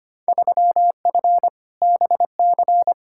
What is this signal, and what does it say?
Morse code, heard as a keyed tone, about 25 wpm, 700 Hz. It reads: 3F BC
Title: Field Day Exchanges Encore - 25wpm